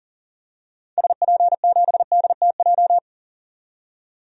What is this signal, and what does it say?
Morse code, heard as a keyed tone, about 39 wpm, 700 Hz. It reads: SP7DTJ
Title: DXCC-100 Call Signs - 6 Characters - 40wpm